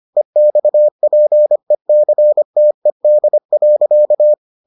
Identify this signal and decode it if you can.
Morse code, heard as a keyed tone, about 25 wpm, 600 Hz. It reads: EXPECTED.